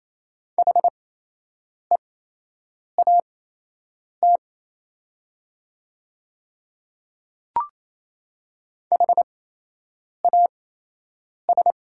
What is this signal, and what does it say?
Morse code, heard as a keyed tone, about 28 wpm, spaced as for 8 wpm, 690 Hz. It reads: HEAT HAS
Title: 8x Character Spacing - Top 500 Words - 28wpm